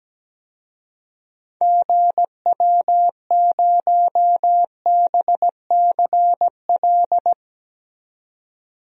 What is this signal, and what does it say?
Morse code, heard as a keyed tone, about 17 wpm, 700 Hz. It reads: GW0BCL
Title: DXCC-100 Call Signs - 6 Characters - 17wpm